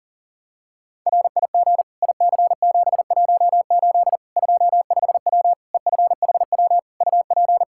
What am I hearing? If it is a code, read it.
Morse code, heard as a keyed tone, about 37 wpm, 700 Hz. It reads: RIG IC718 25W EFHW UP
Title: Practice QSO - Exchange 2 - Info - 40wpm